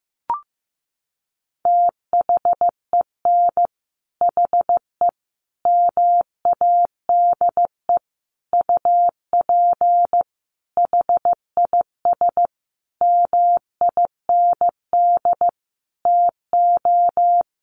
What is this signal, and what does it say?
Morse code, heard as a keyed tone, about 15 wpm, 700 Hz. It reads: THEN HE MADE UP HIS MIND TO
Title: Sentences from Top 1400 Words 15wpm